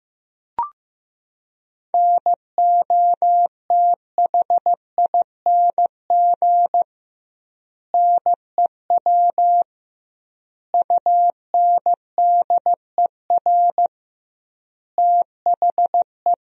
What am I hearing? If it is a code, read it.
Morse code, heard as a keyed tone, about 15 wpm, 700 Hz. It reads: NOTHING NEW UNDER THE
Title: 2x Word-Spacing - Sentences from Top 500 Words - 15wpm